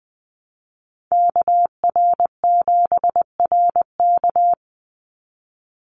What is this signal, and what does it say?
Morse code, heard as a keyed tone, about 20 wpm, 700 Hz. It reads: KR7RK
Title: NA SOTA Activator Callsigns 20wpm